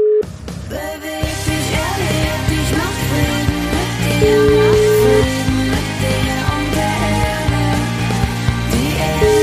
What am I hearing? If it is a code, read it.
Ringback tone (Europe, 425 Hz), with noise.